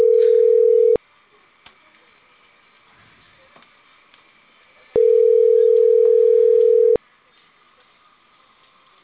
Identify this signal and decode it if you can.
Ringback tone (North America), 440 + 480 Hz.
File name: CENTRAL CAEA / out agcaea02 - .47049